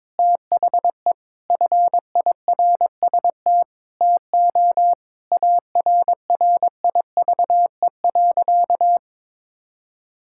Morse code, heard as a keyed tone, about 21 wpm, 700 Hz. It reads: THE FIRST TO ARRIVE.
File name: practice / 22wpm / Sentences with 4-Word Phrases - 22wpm